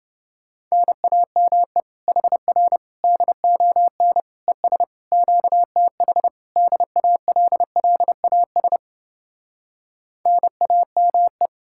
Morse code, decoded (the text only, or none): NAME HR DON ES QTH DALLAS NAME